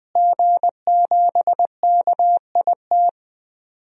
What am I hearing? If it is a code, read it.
Morse code, heard as a keyed tone, about 20 wpm, 700 Hz. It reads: G7KIT
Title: DXCC-100 Call Signs - 5 Characters - Repeated 2x - 20wpm